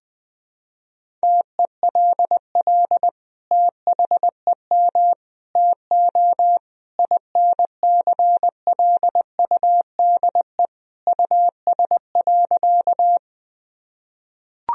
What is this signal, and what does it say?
Morse code, heard as a keyed tone, about 20 wpm, 700 Hz. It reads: TELL THEM TO INCLUDE US.
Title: Sentences from Top 1000 Words - Review Part 3 - 20wpm